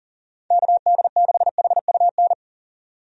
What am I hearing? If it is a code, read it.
Morse code, heard as a keyed tone, about 40 wpm, 700 Hz. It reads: KD6HUN
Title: USA Call Signs - 6 Characters - 40wpm